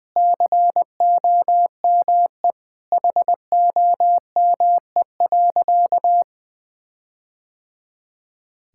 Morse code, heard as a keyed tone, about 20 wpm, 700 Hz. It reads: COME HOME.
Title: Sentences with 4-Word Phrases - Spoken Beginning - 20wpm